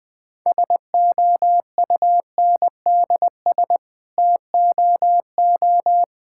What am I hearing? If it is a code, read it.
Morse code, heard as a keyed tone, about 20 wpm, 700 Hz. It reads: SOUNDS TOO
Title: Sets of 9 Words - 20wpm